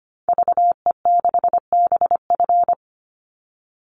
Morse code, heard as a keyed tone, about 25 wpm, 700 Hz. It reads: VE6BF
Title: Call Signs Repeated 2x - 25wpm